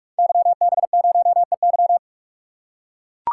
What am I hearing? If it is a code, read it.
Morse code, heard as a keyed tone, about 44 wpm, 700 Hz. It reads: YB0EY
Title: Code Groups - 5 Letter:Numbers - 45wpm